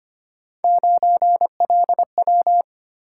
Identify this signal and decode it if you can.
Morse code, heard as a keyed tone, about 25 wpm, 700 Hz. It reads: 9LW
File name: Three Letter-Number Combinations - 25wpm